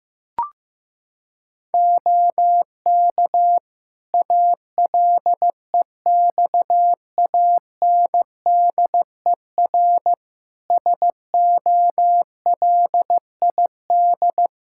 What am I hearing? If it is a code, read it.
Morse code, heard as a keyed tone, about 15 wpm, 700 Hz. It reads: OK ALEXANDER SOLID
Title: K1USN SST QSO Practice 15wpm